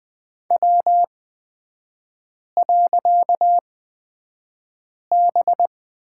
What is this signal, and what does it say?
Morse code, heard as a keyed tone, about 20 wpm, spaced as for 5 wpm, 700 Hz. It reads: W.B
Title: 255 Lesson 41e - Period - Three Characters 20wpm